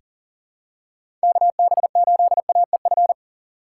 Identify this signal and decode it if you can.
Morse code, heard as a keyed tone, about 40 wpm, 700 Hz. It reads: KB8AEF
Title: USA Call Signs - 6 Characters - 4x Character Spacing - 40wpm